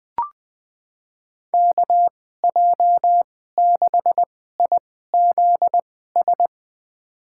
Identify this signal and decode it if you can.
Morse code, heard as a keyed tone, about 20 wpm, spaced as for 14 wpm, 700 Hz. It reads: KJ6IZS
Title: USA Call Signs - 6 Characters - 2x Character Spacing - 20wpm